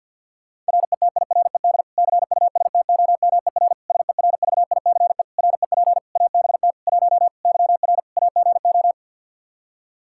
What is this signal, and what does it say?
Morse code, heard as a keyed tone, about 47 wpm, 700 Hz. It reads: RETIRED CUSTOMER SERVICE REP ABT 1 YR AGO
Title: Practice QSO - Exchange 3 - More Info - 50wpm